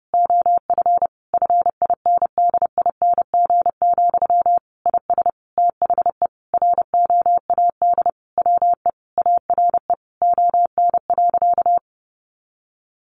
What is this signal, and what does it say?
Morse code, heard as a keyed tone, about 30 wpm, 700 Hz. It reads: OF FINDING, IS THE ROAD WE ARE ON.
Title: Sentences from Top 1100 Words 30wpm